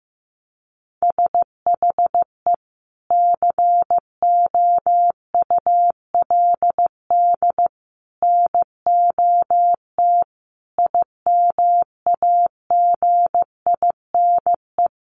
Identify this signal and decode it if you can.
Morse code, heard as a keyed tone, about 15 wpm, 700 Hz. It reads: SHE COULD NOT IMAGINE
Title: Sentences from Top 1400 Words 15wpm